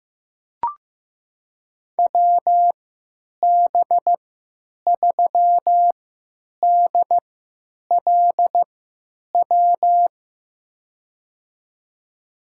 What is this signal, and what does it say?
Morse code, heard as a keyed tone, about 15 wpm, spaced as for 9 wpm, 700 Hz. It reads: WB3DLW
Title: USA Call Signs - 6 Characters - 3x Character Spacing - 15wpm